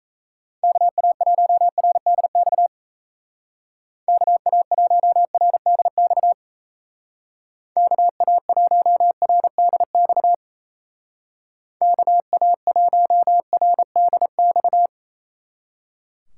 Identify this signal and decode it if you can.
Morse code, heard as a keyed tone, about 35 wpm, 700 Hz. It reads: KA1RDX KA1RDX KA1RDX KA1RDX
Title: Speed Racing - USA 2x3 Call Signs 28wpm